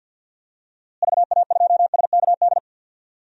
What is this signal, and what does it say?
Morse code, heard as a keyed tone, about 50 wpm, 700 Hz. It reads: VA2SKD
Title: DXCC-100 Call Signs - 6 Characters - Repeated 2x - 50wpm